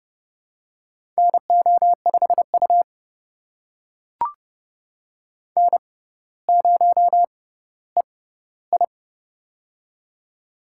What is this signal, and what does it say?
Morse code, heard as a keyed tone, about 30 wpm, spaced as for 15 wpm, 700 Hz. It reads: NO5U N0EI
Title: USA Call Signs - 4 Characters - 6x Character Spacing - 30wpm